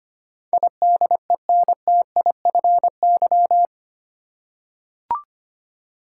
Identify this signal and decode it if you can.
Morse code, heard as a keyed tone, about 25 wpm, 700 Hz. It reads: IDENTIFY
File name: Eight Letter Words - 25wpm